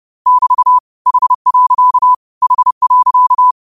Morse code, heard as a keyed tone, about 30 wpm, 1000 Hz. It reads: <BT> H. H.
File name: qotdfiles / 30wpm